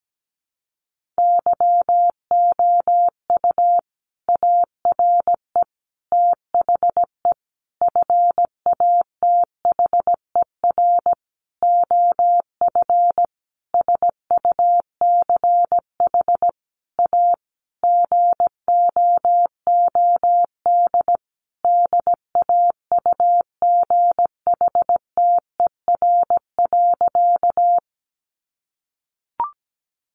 Morse code, heard as a keyed tone, about 17 wpm, 700 Hz. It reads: YOU ARE THE FATHER OF SUCH A GOOD DAUGHTER.
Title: Sentences from Top 1500 Words 17wpm